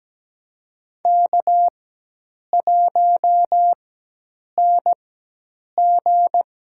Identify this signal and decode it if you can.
Morse code, heard as a keyed tone, about 17 wpm, spaced as for 8 wpm, 700 Hz. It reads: K1NG
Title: USA Call Signs - 4 Characters - 4x Character Spacing - 17wpm